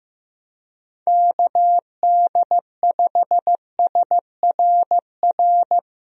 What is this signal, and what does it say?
Morse code, heard as a keyed tone, about 15 wpm, 700 Hz. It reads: KD5SRR
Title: USA Call Signs - 6 Characters - Repeated 3x - 15wpm